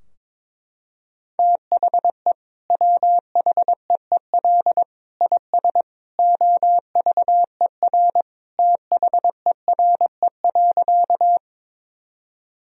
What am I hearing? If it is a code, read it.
Morse code, heard as a keyed tone, about 22 wpm, 700 Hz. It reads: THE WHEEL IS OVER THERE.
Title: Sentences from Top 1000 Words - Review Part 3 - 22wpm